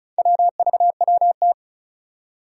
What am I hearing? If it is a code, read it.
Morse code, heard as a keyed tone, about 36 wpm, 700 Hz. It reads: WVWT
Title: Four Challenging Letter-Number Combinations 35wpm